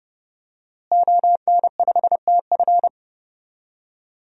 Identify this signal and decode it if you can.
Morse code, heard as a keyed tone, about 31 wpm, 700 Hz. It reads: ON5TF
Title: DXCC-100 Call Signs - 5 Characters - Repeated 2x - 30wpm